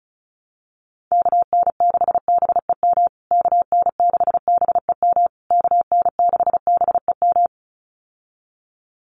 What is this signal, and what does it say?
Morse code, heard as a keyed tone, about 34 wpm, 700 Hz. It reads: KN6BEM KN6BEM KN6BEM
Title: USA Call Signs Repeated 3x 35wpm